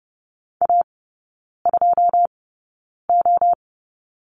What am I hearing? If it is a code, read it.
Morse code, heard as a keyed tone, about 30 wpm, spaced as for 9 wpm, 700 Hz. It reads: A2O
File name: Lesson 16e - Letter c - Three Characters 30wpm